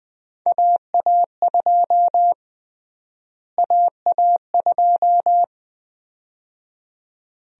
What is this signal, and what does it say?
Morse code, heard as a keyed tone, about 21 wpm, 700 Hz. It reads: AA2 AA2
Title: USA Call Sign Prefixes - Repeated 2x - 20wpm